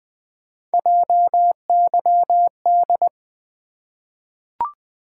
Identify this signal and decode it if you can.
Morse code, heard as a keyed tone, about 20 wpm, 700 Hz. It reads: JYD
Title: Three Letter-Number Combinations - 20wpm